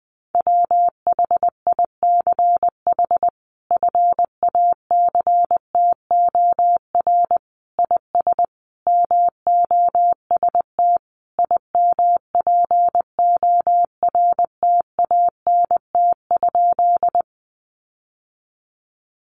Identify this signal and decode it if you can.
Morse code, heard as a keyed tone, about 20 wpm, 700 Hz. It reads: WHICH FACTOR IS MOST IMPORTANT?
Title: Sentences from Top 1300 Words 20wpm